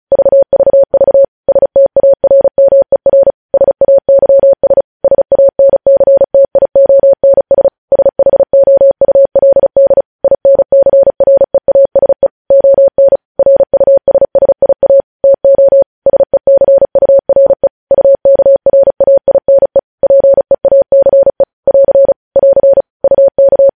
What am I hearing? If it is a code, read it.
Morse code, heard as a keyed tone, about 36 wpm, 550 Hz. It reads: VVV STARMER SAYS SANCTIONS SHOULD INCREASE ON RUSSIA TO SECURE UKRAINE PEACE <AR> <AR> UK
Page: News headlines in Morse code at 35 wpm